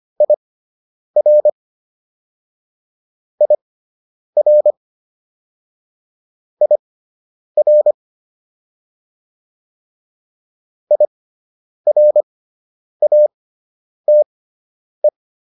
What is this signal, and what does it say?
Morse code, heard as a keyed tone, about 25 wpm, spaced as for 9 wpm, 600 Hz. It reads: IR IR IR IRATE